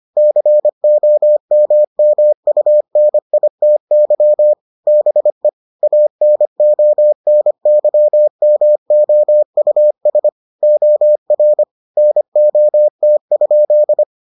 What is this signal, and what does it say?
Morse code, heard as a keyed tone, about 25 wpm, 600 Hz. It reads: COMMUNITY BE ANONYMOUS OR NOT?